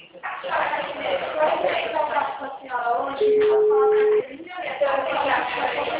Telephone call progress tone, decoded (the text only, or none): ringback tone